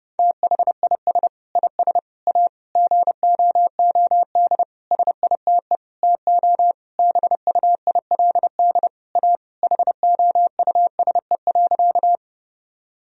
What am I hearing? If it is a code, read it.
Morse code, heard as a keyed tone, about 28 wpm, 700 Hz. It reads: THIS IS A GOOD SITE TO BUILD A HOUSE.
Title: Sentences from Top 1000 Words - Review Part 4 - 30wpm